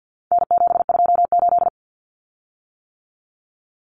Morse code, heard as a keyed tone, about 48 wpm, 700 Hz. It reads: N728
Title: Common Cut Numbers - 4-Digit 50wpm